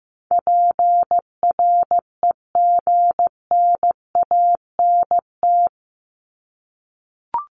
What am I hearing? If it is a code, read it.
Morse code, heard as a keyed tone, about 15 wpm, 700 Hz. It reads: PREGNANT
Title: Top 751-1000 Words 15wpm